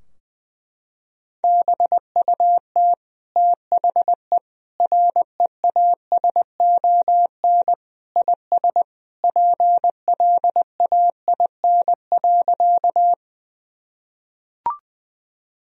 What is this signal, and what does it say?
Morse code, heard as a keyed tone, about 20 wpm, 700 Hz. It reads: BUT THE REASON IS PLAIN.
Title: Sentences from Top 1000 Words - Review Part 2 - 20wpm